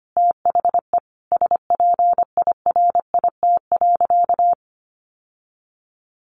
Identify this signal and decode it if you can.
Morse code, heard as a keyed tone, about 24 wpm, 700 Hz. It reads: THE SPIRIT.
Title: Sentences from Top 1400 Words 25wpm